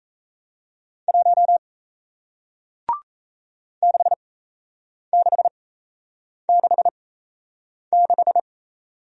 Morse code, heard as a keyed tone, about 38 wpm, spaced as for 8 wpm, 700 Hz. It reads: 1 6666